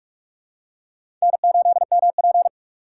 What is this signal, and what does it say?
Morse code, heard as a keyed tone, about 45 wpm, 700 Hz. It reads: N8MP
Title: USA Call Signs - 4x Character Spacing - 45wpm